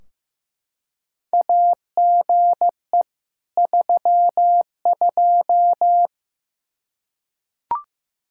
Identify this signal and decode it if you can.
Morse code, heard as a keyed tone, about 15 wpm, 700 Hz. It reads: AGE 32
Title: K1USN SST QSO Practice 15wpm